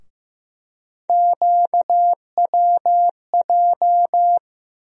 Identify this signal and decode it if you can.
Morse code, heard as a keyed tone, about 15 wpm, 700 Hz. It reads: QWJ
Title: USA Call Sign Postfixes - 3 Letters 15wpm